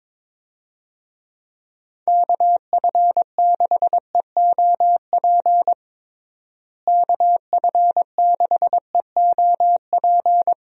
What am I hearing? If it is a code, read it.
Morse code, heard as a keyed tone, about 22 wpm, 700 Hz. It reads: KF6EOP KF6EOP
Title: USA Call Signs - 6 Characters - Repeated 3x - 22wpm